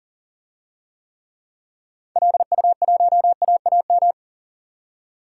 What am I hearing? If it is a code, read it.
Morse code, heard as a keyed tone, about 40 wpm, 700 Hz. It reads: LU1AAM
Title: DXCC-100 Call Signs - 6 Characters - 40wpm